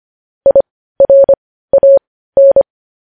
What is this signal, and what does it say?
Morse code, heard as a keyed tone, about 25 wpm, spaced as for 15 wpm, 550 Hz. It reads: IRAN
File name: podcast / News headlines-1x-25x15